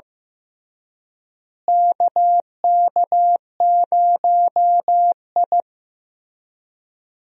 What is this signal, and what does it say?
Morse code, heard as a keyed tone, about 16 wpm, 700 Hz. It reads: KK0I
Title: USA Call Signs - 4 Characters - Repeated 3x - 15wpm